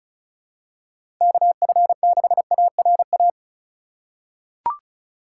Morse code, heard as a keyed tone, about 36 wpm, 700 Hz. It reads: KF6ARA
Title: USA Call Signs - 6 Characters - 4x Character Spacing - 35wpm